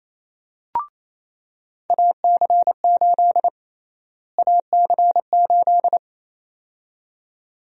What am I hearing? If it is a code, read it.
Morse code, heard as a keyed tone, about 28 wpm, 700 Hz. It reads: AC8 AC8
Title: USA Call Sign Prefixes - Repeated 2x - 28wpm